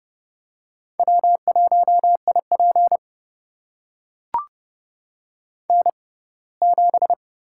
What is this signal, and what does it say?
Morse code, heard as a keyed tone, about 30 wpm, spaced as for 20 wpm, 700 Hz. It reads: W1IP N7
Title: USA Call Signs - 4 Characters - 6x Character Spacing - 30wpm